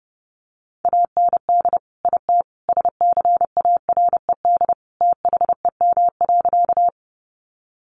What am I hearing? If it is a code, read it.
Morse code, heard as a keyed tone, about 30 wpm, 700 Hz. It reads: AND IT SCARED THEM.
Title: Sentences from Top 700 Words 30wpm